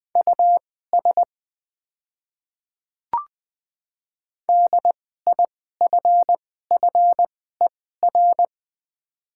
Morse code, heard as a keyed tone, about 20 wpm, spaced as for 14 wpm, 700 Hz. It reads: US DIFFER
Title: 2x Character Spacing - Top 200 Words - 20wpm